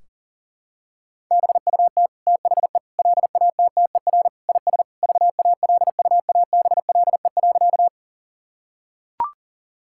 Morse code, heard as a keyed tone, about 37 wpm, 700 Hz. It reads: BUT THE LATTER IS VALUABLE.
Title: Sentences from Top 1600 Words 40wpm